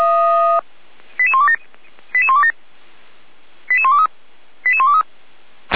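Five-tone selective call (ZVEI), heard as quick runs of five tones, tones of about 70 ms. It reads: 88127 88127 88123 88123
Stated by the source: Five-tone ZVEI selective call received on an SDR.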